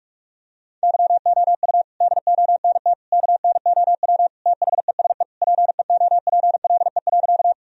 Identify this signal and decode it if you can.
Morse code, heard as a keyed tone, about 45 wpm, 700 Hz. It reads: YOU DONT KNOW THESE PEOPLE.
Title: Sentences from Top 1000 Words - Review Part 1 - 45wpm